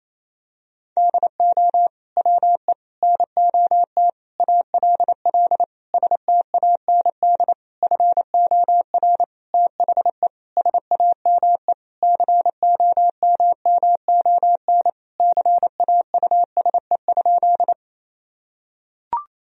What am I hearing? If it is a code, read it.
Morse code, heard as a keyed tone, about 28 wpm, 700 Hz. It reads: DO WE NOT ALL STAND FOR THE SAME COMMON CAUSE?
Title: Sentences from Top 1000 Words - Review Part 3 - 28wpm